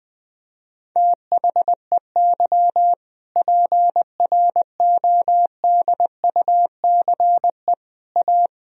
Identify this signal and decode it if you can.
Morse code, heard as a keyed tone, about 21 wpm, 700 Hz. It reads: THEY PRODUCE A
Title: Sentences from Top 1000 Words - Review Part 3 - 20wpm